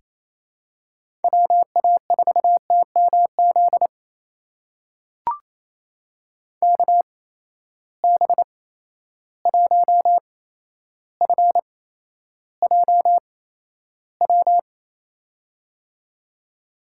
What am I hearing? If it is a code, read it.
Morse code, heard as a keyed tone, about 28 wpm, spaced as for 12 wpm, 700 Hz. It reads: WA4TMZ KB1FJW
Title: USA Call Signs - 6 Characters - 8x Character Spacing - 28wpm